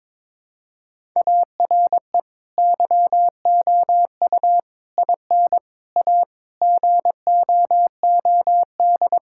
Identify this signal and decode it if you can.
Morse code, heard as a keyed tone, about 22 wpm, 700 Hz. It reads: ARE YOU IN A GOOD
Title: Sentences from Top 1000 Words - Review Part 6 - 22wpm